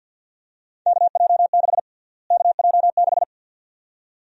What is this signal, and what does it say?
Morse code, heard as a keyed tone, about 53 wpm, 700 Hz. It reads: KJ6 KJ6
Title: USA Call Sign Prefixes - Repeated 2x - 50wpm